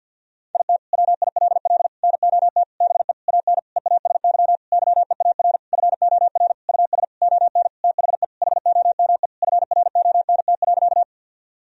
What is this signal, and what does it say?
Morse code, heard as a keyed tone, about 50 wpm, 700 Hz. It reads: IT WILL NOT BE AN EASY YEAR FOR US ON THE HOME FRONT.
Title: Sentences from Top 1000 Words - Review Part 2 - 50wpm